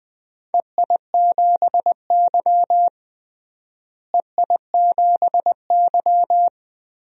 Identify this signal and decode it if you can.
Morse code, heard as a keyed tone, about 20 wpm, 700 Hz. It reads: EI7Y EI7Y
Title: DXCC-100 Call Signs - 4 Characters - Repeated 2x - 20wpm